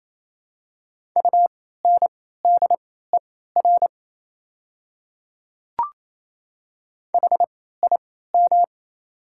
Morse code, heard as a keyed tone, about 27 wpm, spaced as for 16 wpm, 700 Hz. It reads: UNDER HIM
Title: 3x Character Spacing - Top 200 Words - 28wpm